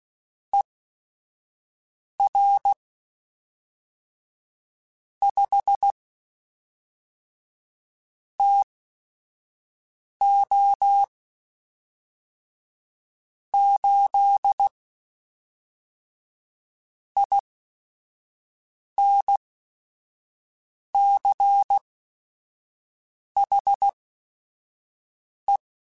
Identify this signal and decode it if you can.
Morse code, heard as a keyed tone, about 16 wpm, spaced as for 5 wpm, 800 Hz. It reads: ER 5 TO 8 INCHE
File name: QSOs / QSOs-mp3s / WA7QOX 1@5